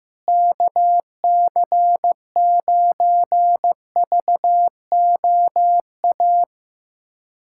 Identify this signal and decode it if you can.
Morse code, heard as a keyed tone, about 15 wpm, 700 Hz. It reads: KC9VOA
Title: USA Callsign DE Callsign - 6 Characters - 15wpm